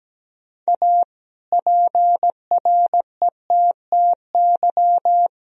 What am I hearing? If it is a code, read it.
Morse code, heard as a keyed tone, about 17 wpm, 700 Hz. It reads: A PRETTY
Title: Sentences from Top 1000 Words - Review Part 2 - 17wpm